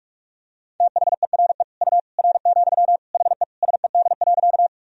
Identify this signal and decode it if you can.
Morse code, heard as a keyed tone, about 46 wpm, 700 Hz. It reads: THERE U R, HE SED.
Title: Sentences from Top 100 Words and Abbreviations 45wpm